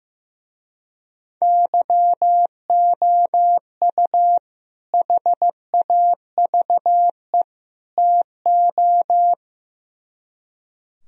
Morse code, heard as a keyed tone, about 15 wpm, 700 Hz. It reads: YOU HAVE TO
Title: Speed-Racing - Sets of 3 Words 15wpm